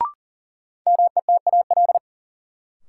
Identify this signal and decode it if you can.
Morse code, heard as a keyed tone, about 40 wpm, 700 Hz. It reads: METAL
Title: Sentences from Top 1000 Words - Review Part 6 - New Word Drill - 40wpm